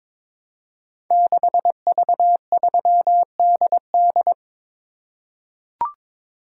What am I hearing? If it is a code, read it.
Morse code, heard as a keyed tone, about 22 wpm, 700 Hz. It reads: 6V3DD
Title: Code Groups - 5 Letter:Numbers - 22wpm